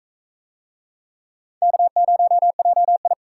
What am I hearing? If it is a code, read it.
Morse code, heard as a keyed tone, about 43 wpm, 700 Hz. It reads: K0JI